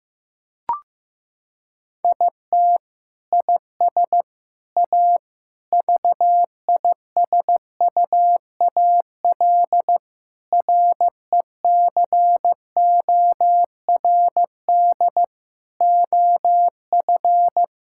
Morse code, heard as a keyed tone, about 15 wpm, 700 Hz. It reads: IT IS A VISUAL RECORD OF
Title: Sentences from Top 1000 Words - Review Part 6 - 15wpm